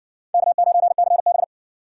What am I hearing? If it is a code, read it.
Morse code, heard as a keyed tone, about 54 wpm, 700 Hz. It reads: K9CB